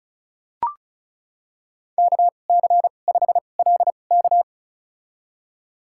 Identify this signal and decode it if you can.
Morse code, heard as a keyed tone, about 36 wpm, spaced as for 25 wpm, 700 Hz. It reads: KC5LK
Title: USA Call Signs - 2x Character Spacing - 35wpm